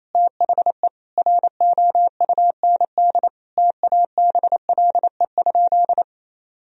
Morse code, heard as a keyed tone, about 28 wpm, 700 Hz. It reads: THE ROUND TABLE?
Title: Sentences from Top 1000 Words - Review Part 2 - 28wpm